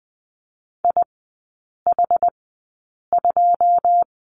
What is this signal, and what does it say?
Morse code, heard as a keyed tone, about 20 wpm, spaced as for 8 wpm, 700 Hz. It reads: IH2
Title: 152 Lesson 25e - Letter p - Three Characters 20wpm